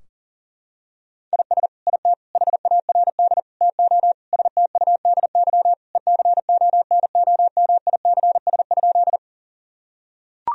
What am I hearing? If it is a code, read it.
Morse code, heard as a keyed tone, about 37 wpm, 700 Hz. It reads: IS IT HARD TO STUDY ECONOMICS?
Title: Sentences from Top 1000 Words - Review Part 3 - 40wpm